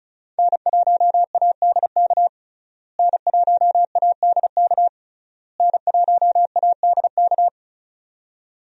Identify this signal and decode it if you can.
Morse code, heard as a keyed tone, about 35 wpm, 700 Hz. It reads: N1ADK N1ADK N1ADK
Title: USA Call Signs - 5 Characters - Repeated 3x - 35wpm